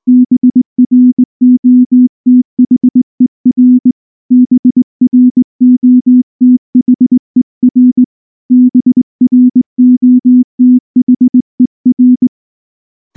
Morse code, heard as a keyed tone, about 19 wpm, 250 Hz. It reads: BROTHER BROTHER BROTHER
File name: morse / morse-training-20WPM-10